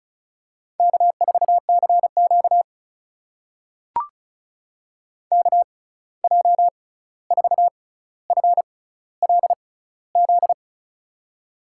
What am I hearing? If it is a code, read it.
Morse code, heard as a keyed tone, about 36 wpm, spaced as for 16 wpm, 700 Hz. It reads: K4CQ KJ4FLZ
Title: USA Call Signs - 6x Character Spacing - 35wpm